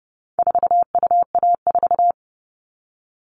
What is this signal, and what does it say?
Morse code, heard as a keyed tone, about 30 wpm, 700 Hz. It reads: 4UA4
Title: All Cut Numbers - 4-Digit 30wpm